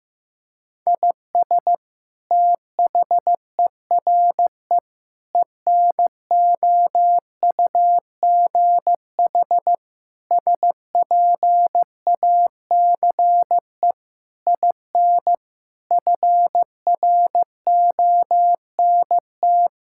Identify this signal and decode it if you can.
Morse code, heard as a keyed tone, about 15 wpm, 700 Hz. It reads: IS THERE ENOUGH SPACE IN FRONT
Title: Sentences from Top 1000 Words - Review Part 5 - 15wpm